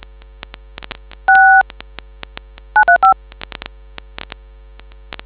DTMF, heard as telephone touch tones, tones of about 120 ms. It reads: 6 935